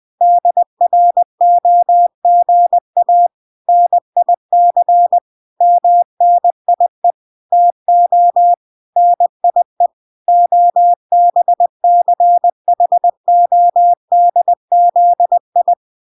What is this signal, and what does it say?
Morse code, heard as a keyed tone, about 20 wpm, 700 Hz. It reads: DROGA NIC MNIE TO NIE OBCHODZI